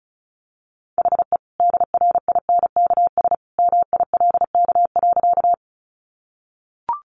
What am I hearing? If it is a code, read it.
Morse code, heard as a keyed tone, about 34 wpm, 700 Hz. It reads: HE DRINKS MILK.
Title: Easy Phrases 35wpm